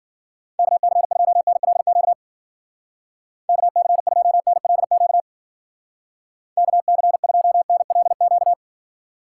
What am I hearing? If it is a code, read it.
Morse code, heard as a keyed tone, about 50 wpm, 700 Hz. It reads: KC2NLQ KC2NLQ KC2NLQ
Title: Speed Racing - USA 2x3 Call Signs 40wpm